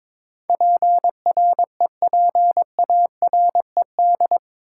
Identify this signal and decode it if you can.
Morse code, heard as a keyed tone, about 22 wpm, 700 Hz. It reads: PREPARED
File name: practice / 22wpm / Eight Letter Words - 22wpm